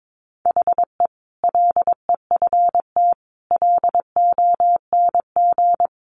Morse code, heard as a keyed tone, about 22 wpm, 700 Hz. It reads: HE LEFT LONG
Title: Sentences from Top 500 Words 22wpm